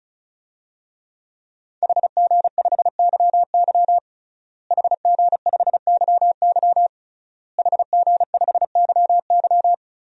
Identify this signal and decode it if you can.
Morse code, heard as a keyed tone, about 34 wpm, 700 Hz. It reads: HG5YY HG5YY HG5YY
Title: DXCC-100 Call Signs - 5 Characters - Repeated 3x - 35wpm